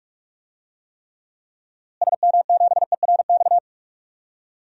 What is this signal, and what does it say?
Morse code, heard as a keyed tone, about 46 wpm, 700 Hz. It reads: SM7ERX
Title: DXCC-100 Call Signs - 6 Characters - Repeated 3x - 45wpm